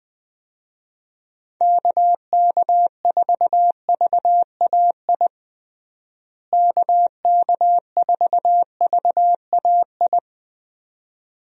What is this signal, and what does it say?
Morse code, heard as a keyed tone, about 20 wpm, 700 Hz. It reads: KK4VAI KK4VAI
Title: USA Call Signs - 6 Characters - Repeated 2x - 20wpm